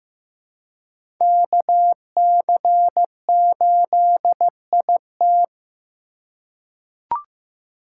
Morse code, heard as a keyed tone, about 15 wpm, 700 Hz. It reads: KC8IT
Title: USA Call Signs - 3x Character Spacing - 15wpm